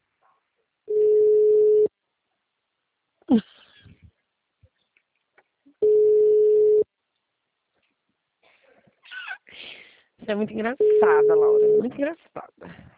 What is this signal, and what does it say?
Ringback tone (Europe, 425 Hz), with speech.
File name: out- -2006- - - .30092